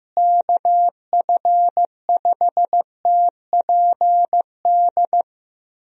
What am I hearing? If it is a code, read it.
Morse code, heard as a keyed tone, about 15 wpm, 700 Hz. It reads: KF5TPD
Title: USA Call Signs - 6 Characters - Repeated 2x - 15wpm